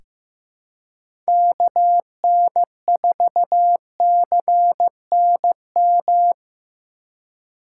Morse code, heard as a keyed tone, about 15 wpm, 700 Hz. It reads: KN4CNM
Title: USA Call Signs - 6 Characters - 2x Character Spacing - 15wpm